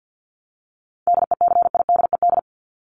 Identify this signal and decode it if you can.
Morse code, heard as a keyed tone, about 50 wpm, 700 Hz. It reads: DECIDED
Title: ED Suffix - Top 100 Words with Base Words 50wpm